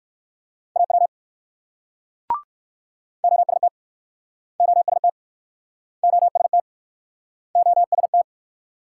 Morse code, heard as a keyed tone, about 62 wpm, 690 Hz. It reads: AW OST OST OST OST